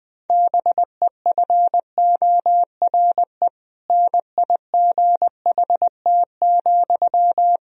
Morse code, heard as a keyed tone, about 21 wpm, 700 Hz. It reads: BEFORE NIGHT,
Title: Sentences from Top 1000 Words - Review Part 1 - 20wpm